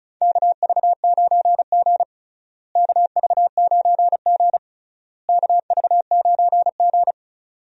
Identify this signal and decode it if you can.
Morse code, heard as a keyed tone, about 34 wpm, 700 Hz. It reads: KV9G KV9G KV9G
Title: USA Call Signs - 4 Characters - Repeated 3x - 35wpm